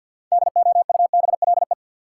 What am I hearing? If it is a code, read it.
Morse code, heard as a keyed tone, about 53 wpm, 700 Hz. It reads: DOUBLE
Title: Sentences from Top 1400 Words - New Word Drill - 50wpm